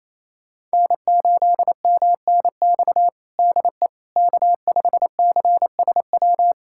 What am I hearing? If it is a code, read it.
Morse code, heard as a keyed tone, about 28 wpm, 700 Hz. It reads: N8MNX DE K5CSW
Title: USA Callsign DE Callsign 28wpm